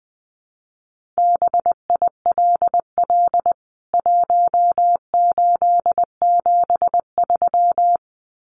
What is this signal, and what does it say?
Morse code, heard as a keyed tone, about 20 wpm, 700 Hz. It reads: BILL 1873
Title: CWT Exchange 20wpm